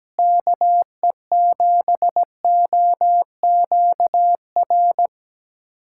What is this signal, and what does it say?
Morse code, heard as a keyed tone, about 17 wpm, 700 Hz. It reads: KE7OQR
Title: USA Callsign DE Callsign - 6 Characters - 17wpm